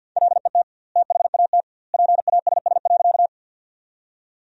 Morse code, heard as a keyed tone, about 52 wpm, 700 Hz. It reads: LET THAT PASS.
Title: 2x Word-Spacing - Sentences from Top 400 Words - 50wpm